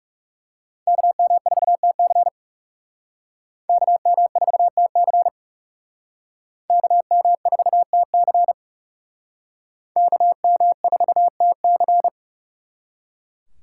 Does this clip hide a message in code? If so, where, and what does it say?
Yes: Morse code from 0.9 to 12.1 s, heard as a keyed tone, about 39 wpm, 700 Hz. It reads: KM4TC KM4TC KM4TC KM4TC